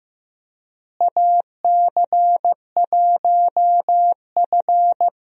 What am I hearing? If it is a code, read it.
Morse code, heard as a keyed tone, about 15 wpm, 700 Hz. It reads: AC1F
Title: USA Call Signs - 4 Characters - Repeated 2x - 15wpm